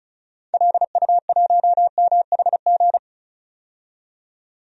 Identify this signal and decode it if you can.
Morse code, heard as a keyed tone, about 36 wpm, 700 Hz. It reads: LU1MHG
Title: DXCC-100 Call Signs - 6 Characters - 35wpm